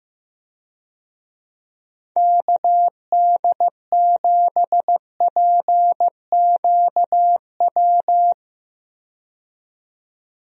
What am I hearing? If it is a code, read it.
Morse code, heard as a keyed tone, about 15 wpm, 700 Hz. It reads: KD7PQW
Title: USA Call Signs - 6 Characters - 15wpm